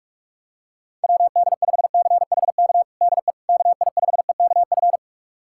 Morse code, heard as a keyed tone, about 46 wpm, 700 Hz. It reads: WD5CHK DE KI5EKF
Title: USA Callsign DE Callsign 45wpm